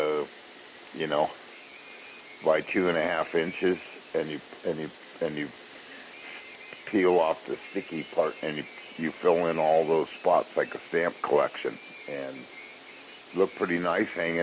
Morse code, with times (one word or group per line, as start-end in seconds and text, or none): none